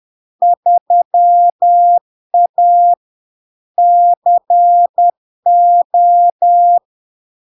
Morse code, heard as a keyed tone, about 10 wpm, 700 Hz. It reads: 3A CO
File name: LARK-FD-2022-Log-in-CW-10wpm-700-Hz